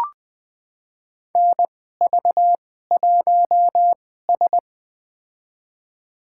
Morse code, heard as a keyed tone, about 20 wpm, spaced as for 14 wpm, 700 Hz. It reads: NV1S
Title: USA Call Signs - 4 Characters - 2x Character Spacing - 20wpm